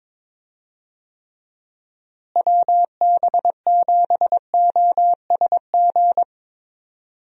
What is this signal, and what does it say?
Morse code, heard as a keyed tone, about 22 wpm, 700 Hz. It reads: WB7OSG
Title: USA Call Signs - 6 Characters - Repeated 3x - 22wpm